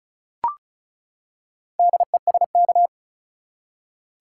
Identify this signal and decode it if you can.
Morse code, heard as a keyed tone, about 36 wpm, 700 Hz. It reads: DESK
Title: Four Letter Words - 35wpm